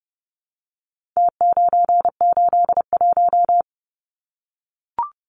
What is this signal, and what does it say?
Morse code, heard as a keyed tone, about 30 wpm, 700 Hz. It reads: T981
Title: Numbers - Contest Style - 4-Digit 30wpm